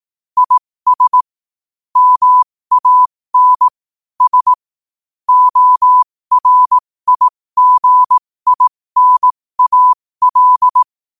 Morse code, heard as a keyed tone, about 18 wpm, spaced as for 15 wpm, 1000 Hz. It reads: IS MAN S ORIGINAL
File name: qotdfiles / 15wpm